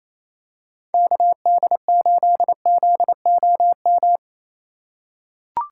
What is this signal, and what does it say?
Morse code, heard as a keyed tone, about 28 wpm, 700 Hz. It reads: KD8ZOM
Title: USA Call Signs - 6 Characters - 4x Character Spacing - 28wpm